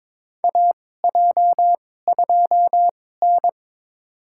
Morse code, heard as a keyed tone, about 22 wpm, spaced as for 16 wpm, 700 Hz. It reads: AJ2N
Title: USA Call Signs - 4 Characters - 2x Character Spacing - 22wpm